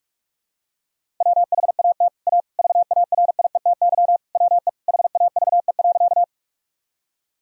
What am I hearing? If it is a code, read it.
Morse code, heard as a keyed tone, about 42 wpm, 700 Hz. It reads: WHAT A VARIETY WE HAVE.
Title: Sentences from Top 1000 Words - Review Part 4 - 45wpm